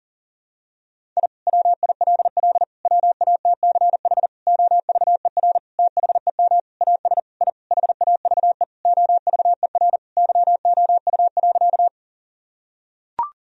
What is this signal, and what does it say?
Morse code, heard as a keyed tone, about 37 wpm, 700 Hz. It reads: I WILL WATCH OVER THEM AS I HAVE OVER YOU.
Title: Sentences from Top 1000 Words - Review Part 1 - 40wpm